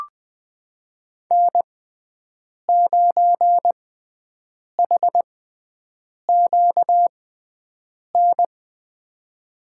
Morse code, heard as a keyed tone, about 21 wpm, spaced as for 7 wpm, 700 Hz. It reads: N9HQN
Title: USA Call Signs - 5 Characters - 6x Character Spacing - 20wpm